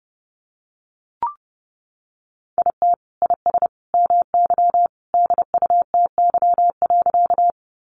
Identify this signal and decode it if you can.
Morse code, heard as a keyed tone, about 30 wpm, 700 Hz. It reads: IT IS MY DUTY.
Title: Sentences from Top 1500 Words 30wpm